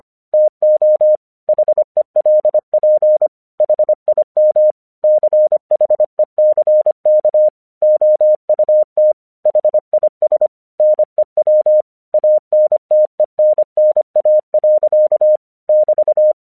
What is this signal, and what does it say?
Morse code, heard as a keyed tone, about 24 wpm, 610 Hz. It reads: TO HELP HIM CHECK OUT HIS NEW ANTENNA. <BT>
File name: 1 English 25wpm / AQC36